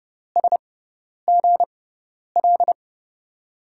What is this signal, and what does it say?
Morse code, heard as a keyed tone, about 30 wpm, spaced as for 10 wpm, 700 Hz. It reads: SGL
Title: USA Call Signs - 6 Characters - 6x Character Spacing - 30wpm